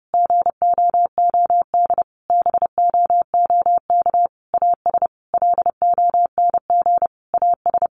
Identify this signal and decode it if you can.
Morse code, heard as a keyed tone, about 30 wpm, 700 Hz. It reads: GOOD BOOK AS LONG AS
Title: Sentences from Top 1200 Words 30wpm